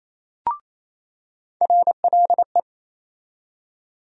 Morse code, heard as a keyed tone, about 28 wpm, 700 Hz. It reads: RLE
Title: USA Call Sign Postfixes - 3 Letters 28wpm